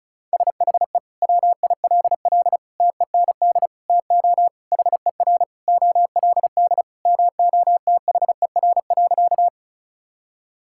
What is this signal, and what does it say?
Morse code, heard as a keyed tone, about 36 wpm, 700 Hz. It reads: SHE WILL TEND TO HER OLD MOTHER.
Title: Sentences from Top 1000 Words - Review Part 5 - 35wpm